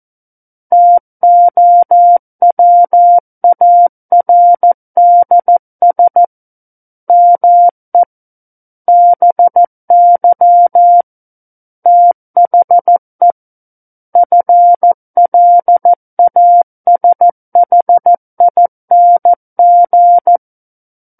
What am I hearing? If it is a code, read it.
Morse code, heard as a keyed tone, about 14 wpm, 700 Hz. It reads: TOWARDS ME BY THE FLASHING
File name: Book One / 05-Chapter 5 - 14 WPM